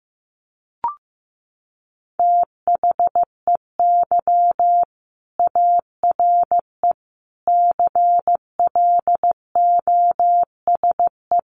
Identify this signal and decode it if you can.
Morse code, heard as a keyed tone, about 15 wpm, 700 Hz. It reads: THEY ARE CLOSE
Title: Sentences from Top 1400 Words 15wpm